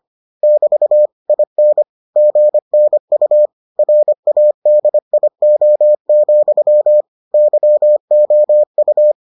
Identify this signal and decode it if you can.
Morse code, heard as a keyed tone, about 25 wpm, 600 Hz. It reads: <BT> IN GNU RADIO, YOU